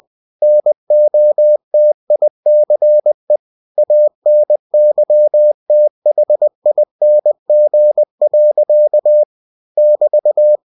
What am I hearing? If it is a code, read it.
Morse code, heard as a keyed tone, about 20 wpm, 600 Hz. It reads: NOTICE ANYTHING. <BT>
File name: cw-books / winnie-the-pooh / winnie-20wpm-page-0065